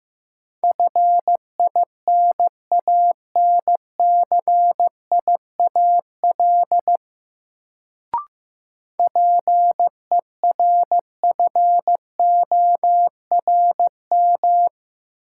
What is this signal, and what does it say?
Morse code, heard as a keyed tone, about 15 wpm, 700 Hz. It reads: FINANCIAL PERFORM
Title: Sentences from Top 1000 Words - Review Part 5 - New Word Drill - 15wpm